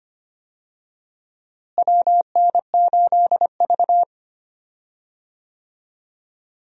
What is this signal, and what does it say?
Morse code, heard as a keyed tone, about 25 wpm, 700 Hz. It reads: WN8V
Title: USA Call Signs - 4 Characters - Repeated 2x - 25wpm